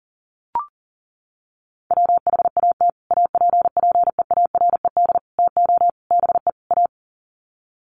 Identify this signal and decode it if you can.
Morse code, heard as a keyed tone, about 40 wpm, 700 Hz. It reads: WHAT APPEARED TO BE A
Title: Sets of 5 Words Encore2 40wpm